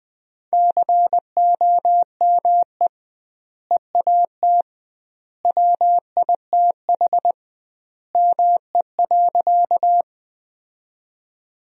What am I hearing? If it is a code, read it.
Morse code, heard as a keyed tone, about 21 wpm, 700 Hz. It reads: COME EAT WITH ME.
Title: 2x Word-Spacing - Sentences from Top 300 Words - 20wpm